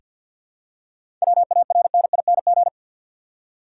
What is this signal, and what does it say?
Morse code, heard as a keyed tone, about 50 wpm, 700 Hz. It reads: WARNING
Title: Seven Letter Words - 50wpm